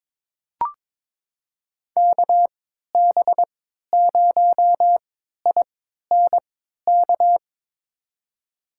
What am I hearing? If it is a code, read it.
Morse code, heard as a keyed tone, about 22 wpm, spaced as for 12 wpm, 700 Hz. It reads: KB0INK
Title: USA Call Signs - 6 Characters - 3x Character Spacing - 22wpm